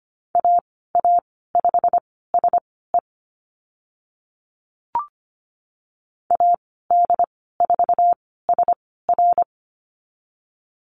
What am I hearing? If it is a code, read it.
Morse code, heard as a keyed tone, about 25 wpm, spaced as for 16 wpm, 690 Hz. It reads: AA5SE AD4SR
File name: Lesson 15g - Number 5 - Call Signs 25wpm